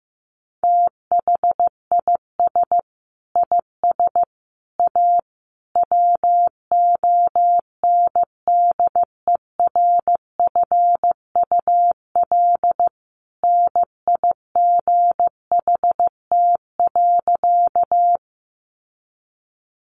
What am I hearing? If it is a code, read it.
Morse code, heard as a keyed tone, about 15 wpm, 700 Hz. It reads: THIS IS A WONDERFUL NIGHT.
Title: Sentences from Top 1100 Words 15wpm